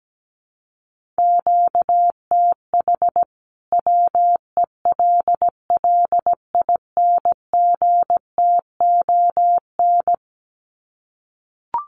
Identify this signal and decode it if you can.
Morse code, heard as a keyed tone, about 17 wpm, 700 Hz. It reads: QTH WELLINGTON
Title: Common QSO Phrases 17wpm